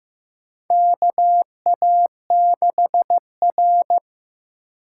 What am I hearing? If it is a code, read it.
Morse code, heard as a keyed tone, about 15 wpm, 700 Hz. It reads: KA6R
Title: USA Call Signs - 4 Characters - 3x Character Spacing - 15wpm